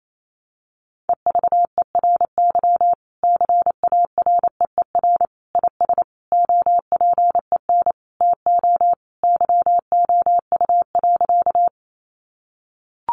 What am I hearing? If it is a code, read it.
Morse code, heard as a keyed tone, about 28 wpm, 700 Hz. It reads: EVERY CAREER IS OPEN TO YOU.
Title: Sentences from Top 1000 Words 28wpm